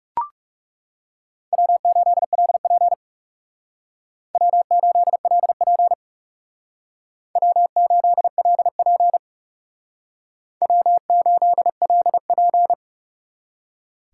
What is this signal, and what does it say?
Morse code, heard as a keyed tone, about 37 wpm, 700 Hz. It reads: W8LP W8LP W8LP W8LP
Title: Speed Racing - USA 1x2 Call Signs 30wpm